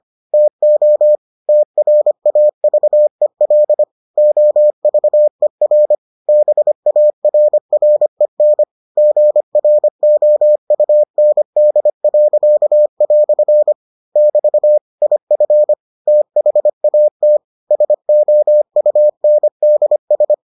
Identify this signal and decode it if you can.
Morse code, heard as a keyed tone, about 25 wpm, 600 Hz. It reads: TO TRAVEL OVER BARREN GROUND." <BT> IF THAT SOUNDS